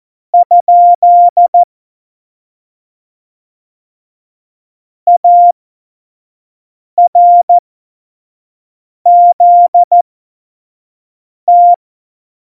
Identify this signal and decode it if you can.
Morse code, heard as a keyed tone, about 14 wpm, spaced as for 5 wpm, 700 Hz. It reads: ? ARZT